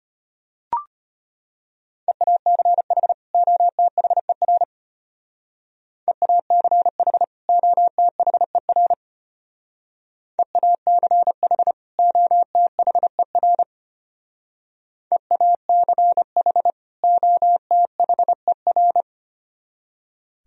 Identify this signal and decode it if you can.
Morse code, heard as a keyed tone, about 31 wpm, 700 Hz. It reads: EACH OTHER EACH OTHER EACH OTHER EACH OTHER
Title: Speed-Racing - Sets of 2 Words 25wpm